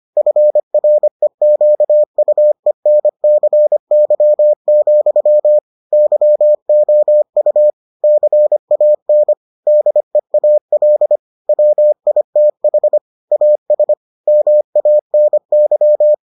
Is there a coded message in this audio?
Yes: Morse code from 0.2 to 16.1 s, heard as a keyed tone, about 25 wpm, 600 Hz. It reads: FREQUENCY, YOU CAN DEAL WITH AS MANY